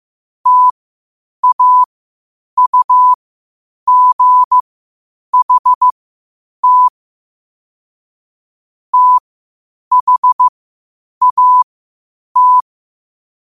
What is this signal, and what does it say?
Morse code, heard as a keyed tone, about 15 wpm, spaced as for 8 wpm, 1000 Hz. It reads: TAUGHT THAT
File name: qotdfiles / 8wpm